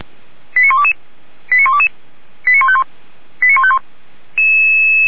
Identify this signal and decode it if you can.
Five-tone selective call (ZVEI), heard as quick runs of five tones, tones of about 70 ms. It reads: 89122 89122 89151 89151